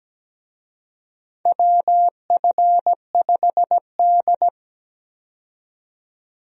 Morse code, heard as a keyed tone, about 17 wpm, 700 Hz. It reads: WF5D
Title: USA Call Signs - 4 Characters - 17wpm